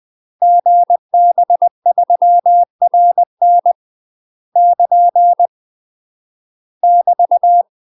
Morse code, heard as a keyed tone, about 20 wpm, 700 Hz. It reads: GB3RN <KN> <BT>
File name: UK MASTER QSOs TRAINING / 20WPM / UK MASTER QSOs 0027